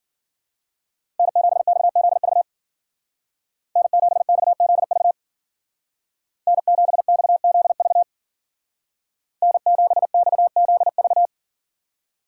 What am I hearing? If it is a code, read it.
Morse code, heard as a keyed tone, about 50 wpm, 700 Hz. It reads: N7XZV N7XZV N7XZV N7XZV
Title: Speed Racing - USA 1x3 Call Signs 40wpm